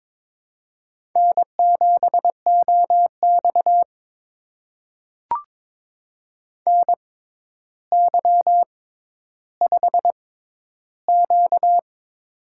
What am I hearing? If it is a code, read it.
Morse code, heard as a keyed tone, about 22 wpm, spaced as for 11 wpm, 700 Hz. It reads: N7OX NY5Q
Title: USA Call Signs - 4 Characters - 6x Character Spacing - 22wpm